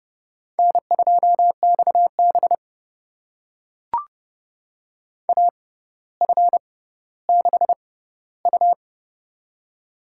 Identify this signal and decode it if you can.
Morse code, heard as a keyed tone, about 30 wpm, spaced as for 15 wpm, 700 Hz. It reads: N2XB AF6U
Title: USA Call Signs - 6x Character Spacing - 30wpm